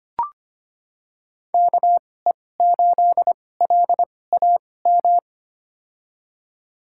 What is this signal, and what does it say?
Morse code, heard as a keyed tone, about 25 wpm, spaced as for 18 wpm, 700 Hz. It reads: KE8LAM
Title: USA Call Signs - 6 Characters - 2x Character Spacing - 25wpm